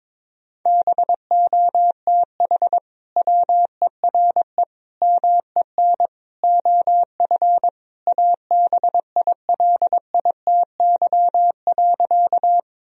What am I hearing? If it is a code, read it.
Morse code, heard as a keyed tone, about 22 wpm, 700 Hz. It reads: BOTH WERE MEN OF ABILITY.
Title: Sentences from Top 1000 Words - Review Part 3 - 22wpm